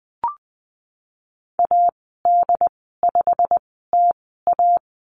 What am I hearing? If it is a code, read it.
Morse code, heard as a keyed tone, about 20 wpm, spaced as for 15 wpm, 700 Hz. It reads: AD5TA
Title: 088 Lesson 15g - Number 5 - Call Signs 20wpm